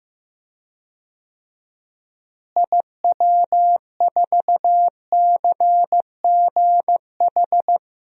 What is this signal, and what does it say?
Morse code, heard as a keyed tone, about 15 wpm, 700 Hz. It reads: IW4CGH
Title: DXCC-100 Call Signs - 6 Characters - Repeated 3x - 15wpm